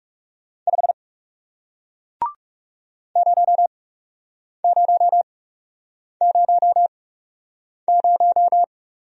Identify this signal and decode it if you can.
Morse code, heard as a keyed tone, about 41 wpm, spaced as for 8 wpm, 700 Hz. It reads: 5 0000